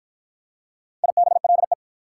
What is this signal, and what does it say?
Morse code, heard as a keyed tone, about 52 wpm, 690 Hz. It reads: IBLE